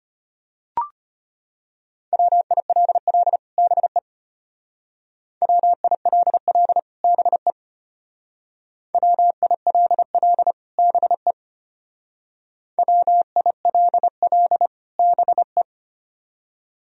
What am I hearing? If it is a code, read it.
Morse code, heard as a keyed tone, about 31 wpm, 700 Hz. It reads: WILL BE WILL BE WILL BE WILL BE
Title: Speed-Racing - Sets of 2 Words 25wpm